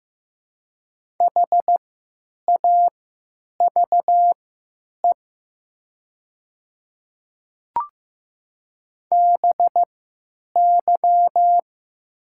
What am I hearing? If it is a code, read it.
Morse code, heard as a keyed tone, about 15 wpm, spaced as for 9 wpm, 700 Hz. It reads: HAVE BY
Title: 3x Character Spacing - Top 100 Words - 15wpm